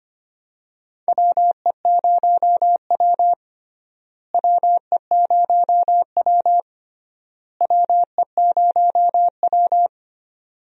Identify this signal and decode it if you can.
Morse code, heard as a keyed tone, about 25 wpm, 700 Hz. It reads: WE0W WE0W WE0W
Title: USA Call Signs - 4 Characters - Repeated 3x - 25wpm